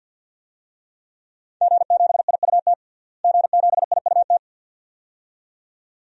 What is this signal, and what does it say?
Morse code, heard as a keyed tone, about 45 wpm, 700 Hz. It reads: G7IUT G7IUT
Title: DXCC-100 Call Signs - 5 Characters - Repeated 2x - 50wpm